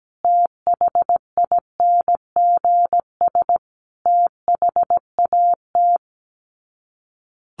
Morse code, heard as a keyed tone, about 17 wpm, 700 Hz. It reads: THINGS THAT
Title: Sets of 5 Words 17wpm